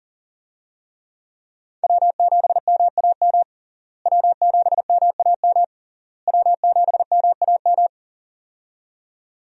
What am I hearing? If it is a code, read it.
Morse code, heard as a keyed tone, about 37 wpm, 700 Hz. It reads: W7MAM W7MAM W7MAM
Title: USA Call Signs - 5 Characters - Repeated 3x - 40wpm